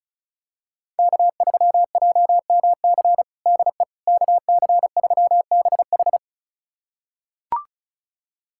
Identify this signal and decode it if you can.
Morse code, heard as a keyed tone, about 36 wpm, 700 Hz. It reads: K3JMC DE KC3BH
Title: USA Callsign DE Callsign - 5 Characters - 35wpm